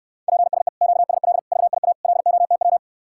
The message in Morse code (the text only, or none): PSE QSL VIA LOTW